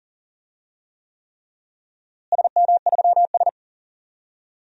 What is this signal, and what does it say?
Morse code, heard as a keyed tone, about 40 wpm, 700 Hz. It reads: SM3S
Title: DXCC-100 Call Signs - 4 Characters - Repeated 3x - 40wpm